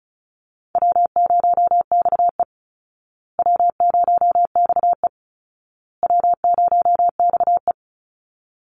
Morse code, heard as a keyed tone, about 34 wpm, 700 Hz. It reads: W0XE W0XE W0XE
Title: Easy Call Signs Repeated 3x - 35wpm